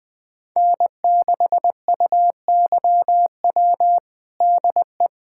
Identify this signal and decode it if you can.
Morse code, heard as a keyed tone, about 20 wpm, 700 Hz. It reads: N6UYW DE
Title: USA Callsign DE Callsign - 5 Characters - 20wpm